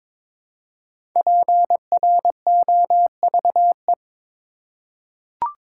Morse code, heard as a keyed tone, about 22 wpm, 700 Hz. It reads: PROVE
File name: 5 Letter Words - First 3 Letters Given - 22wpm